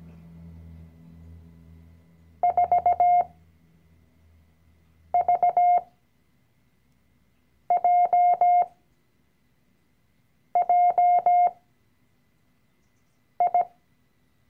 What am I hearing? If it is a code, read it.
Morse code, heard as a keyed tone, about 17 wpm, spaced as for 4 wpm, 710 Hz. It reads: 4VJJI
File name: CW-Online-Kurs / Anfänger / Lektion 012